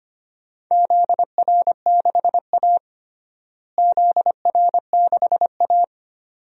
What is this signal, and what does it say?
Morse code, heard as a keyed tone, about 25 wpm, 700 Hz. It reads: ZR6A ZR6A
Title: DXCC-100 Call Signs - 4 Characters - Repeated 2x - 25wpm